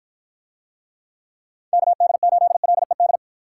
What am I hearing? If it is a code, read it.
Morse code, heard as a keyed tone, about 53 wpm, 700 Hz. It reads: KD8LED